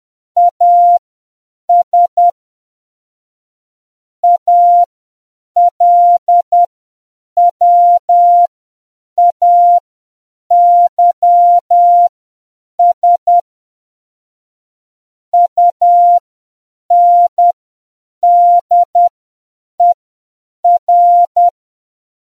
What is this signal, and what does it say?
Morse code, heard as a keyed tone, about 10 wpm, spaced as for 7 wpm, 700 Hz. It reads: AS ALWAYS UNDER